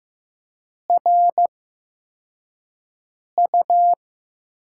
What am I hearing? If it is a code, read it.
Morse code, heard as a keyed tone, about 15 wpm, spaced as for 4 wpm, 700 Hz. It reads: RU
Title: USA Call Signs - 6 Characters - 8x Character Spacing - 15wpm